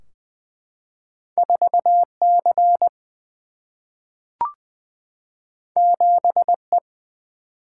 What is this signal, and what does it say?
Morse code, heard as a keyed tone, about 20 wpm, 700 Hz. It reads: 4C 7E
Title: Single-Letter Single-Number 20wpm